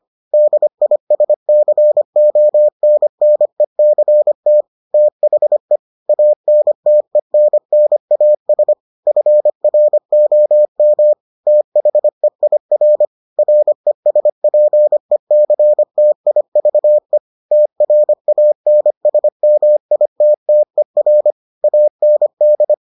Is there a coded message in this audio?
Yes: Morse code from 0.3 to 22.7 s, heard as a keyed tone, about 25 wpm, 600 Hz. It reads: DISCONNECT THE ANTENNAS FROM THEIR RESPECTIVE TRANSMITTER AND